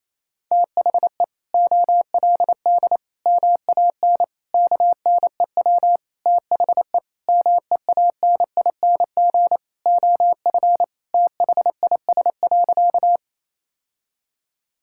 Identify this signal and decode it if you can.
Morse code, heard as a keyed tone, about 28 wpm, 700 Hz. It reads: THE OLD MAN KNEW THE MEANING OF THIS.
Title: Sentences from Top 1000 Words - Review Part 6 - 28wpm